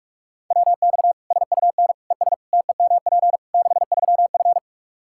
Morse code, heard as a keyed tone, about 43 wpm, 700 Hz. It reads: WX SUN ES TEMP 63F
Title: Practice QSO - Exchange 2 - Info - 45wpm